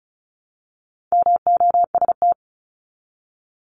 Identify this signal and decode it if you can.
Morse code, heard as a keyed tone, about 34 wpm, 700 Hz. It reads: MOST
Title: Top 200 Words 35wpm